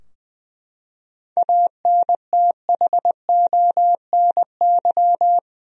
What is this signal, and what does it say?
Morse code, heard as a keyed tone, about 20 wpm, 700 Hz. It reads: ANTHONY
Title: USA Common Names - 20wpm